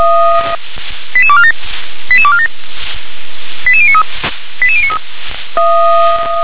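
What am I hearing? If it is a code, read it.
Five-tone selective call (ZVEI), heard as quick runs of five tones, tones of about 70 ms. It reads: 88247 88247 88093 88093
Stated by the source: ZVEI five-tone selective call 88126 received by SDR.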